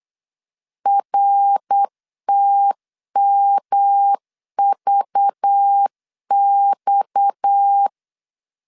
Morse code, heard as a keyed tone, about 8 wpm, 800 Hz. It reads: RTMVX
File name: LEZIONI / 6 PKX0 / 6 PKX 8wpm 3041